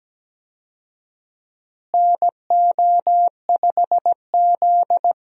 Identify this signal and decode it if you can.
Morse code, heard as a keyed tone, about 17 wpm, 700 Hz. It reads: NO5Z
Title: USA Call Signs - 4 Characters - Repeated 3x - 17wpm